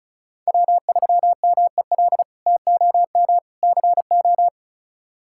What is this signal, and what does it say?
Morse code, heard as a keyed tone, about 34 wpm, 700 Hz. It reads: W3MEL TOM CO
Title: K1USN SST QSO Practice - 35wpm